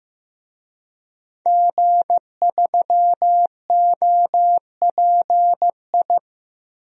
Morse code, heard as a keyed tone, about 15 wpm, 700 Hz. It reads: G3OPI
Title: DXCC-100 Call Signs - 5 Characters - 15wpm